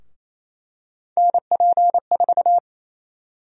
Morse code, heard as a keyed tone, about 28 wpm, 700 Hz. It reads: NP4
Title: USA Call Sign Prefixes - Repeated 2x - 28wpm